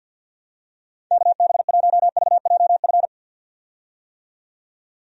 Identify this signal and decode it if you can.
Morse code, heard as a keyed tone, about 52 wpm, 700 Hz. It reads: KB1VJF
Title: USA Call Signs - 6 Characters - 50wpm